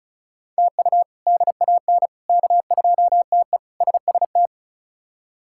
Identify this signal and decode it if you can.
Morse code, heard as a keyed tone, about 34 wpm, 700 Hz. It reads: TU DAN K2TE SST
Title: K1USN SST QSO Practice - 35wpm